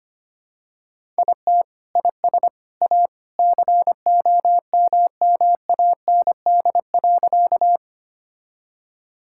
Morse code, heard as a keyed tone, about 25 wpm, 700 Hz. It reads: IT IS A COMMAND.
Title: Sentences from Top 1600 Words 25wpm